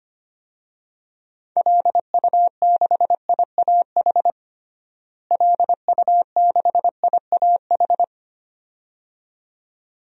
Morse code, heard as a keyed tone, about 25 wpm, 700 Hz. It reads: LU6IAH LU6IAH
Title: DXCC-100 Call Signs - 6 Characters - Repeated 2x - 25wpm